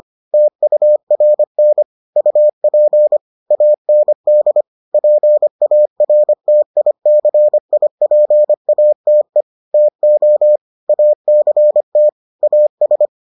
Morse code, heard as a keyed tone, about 25 wpm, 600 Hz. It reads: TURN UP AND PARTICIPATE TO ACT AS